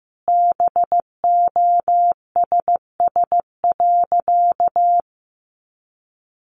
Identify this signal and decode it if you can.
Morse code, heard as a keyed tone, about 15 wpm, 700 Hz. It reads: BOSS.
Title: Sentences from Top 700 Words 15wpm